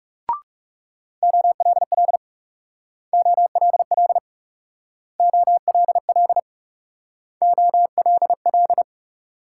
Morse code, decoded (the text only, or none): OLL OLL OLL OLL